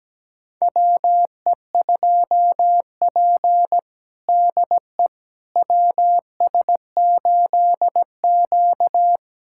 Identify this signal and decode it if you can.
Morse code, heard as a keyed tone, about 17 wpm, 700 Hz. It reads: WE2P DE WS8Q
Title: USA Callsign DE Callsign - 4 Characters - 17wpm